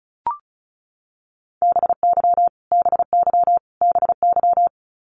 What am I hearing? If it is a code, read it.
Morse code, heard as a keyed tone, about 34 wpm, 700 Hz. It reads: BY BY BY
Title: Top 300 Words Repeated 35wpm